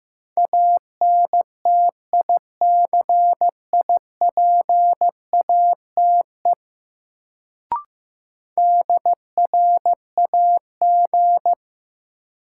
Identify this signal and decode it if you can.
Morse code, heard as a keyed tone, about 15 wpm, 700 Hz. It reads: ANTICIPATE DRAG
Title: Sentences from Top 1500 Words - New Word Drill - 15wpm